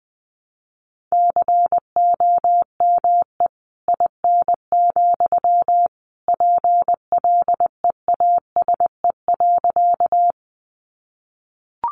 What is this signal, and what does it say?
Morse code, heard as a keyed tone, about 20 wpm, 700 Hz. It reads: COME IN, PLEASE.
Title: Sentences from Top 1500 Words 20wpm